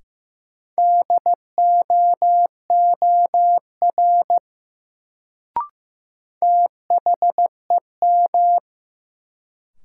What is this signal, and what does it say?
Morse code, heard as a keyed tone, about 15 wpm, 700 Hz. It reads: DOOR THEM
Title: Sets of 2 Words - New Word Drill - 15wpm